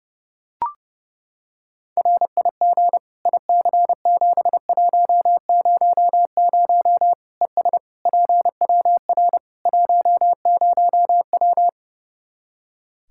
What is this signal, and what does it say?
Morse code, heard as a keyed tone, about 30 wpm, 700 Hz. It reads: RIG IC7100 ES PWR 10W
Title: K1USN SST QSO Practice 30wpm